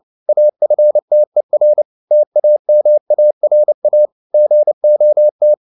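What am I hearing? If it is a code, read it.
Morse code, heard as a keyed tone, about 29 wpm, 600 Hz. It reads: AFTER TAMARA GOT
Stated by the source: Morse code at 29 wpm, sending Ken Follett's novel Never in English, from CW med Gnister.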